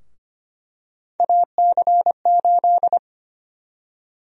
Morse code, heard as a keyed tone, about 25 wpm, 700 Hz. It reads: AC8
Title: USA Call Sign Prefixes - 25wpm